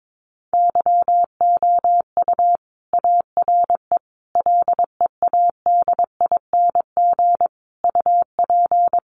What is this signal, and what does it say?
Morse code, heard as a keyed tone, about 22 wpm, 700 Hz. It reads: YOU ARE LEADING UP
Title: Sentences from Top 1100 Words 22wpm